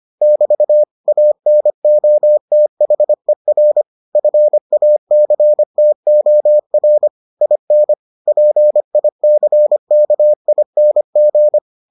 Morse code, heard as a keyed tone, about 25 wpm, 600 Hz. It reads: <BT> ANOTHER FACTOR IN PICKING